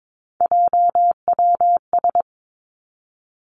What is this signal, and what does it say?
Morse code, heard as a keyed tone, about 22 wpm, 700 Hz. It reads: JWS
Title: Three Challenging Letter-Number Combinations 22wpm